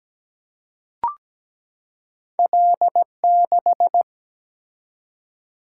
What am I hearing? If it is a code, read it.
Morse code, heard as a keyed tone, about 17 wpm, 700 Hz. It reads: L6
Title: Single-Letter Single-Number 17wpm